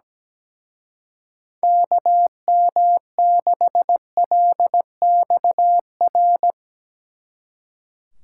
Morse code, heard as a keyed tone, about 18 wpm, 700 Hz. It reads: KM6LXR
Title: Speed Racing - USA 2x3 Call Signs 17wpm